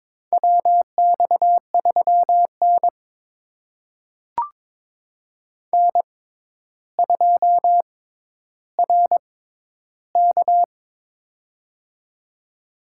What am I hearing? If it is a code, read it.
Morse code, heard as a keyed tone, about 22 wpm, spaced as for 11 wpm, 700 Hz. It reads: WX3N N2RK
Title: USA Call Signs - 4 Characters - 6x Character Spacing - 22wpm